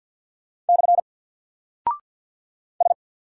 Morse code, heard as a keyed tone, about 49 wpm, 700 Hz. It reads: / S
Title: Single Character (Letter:Number:Punctuation:Prosign) Alt 50wpm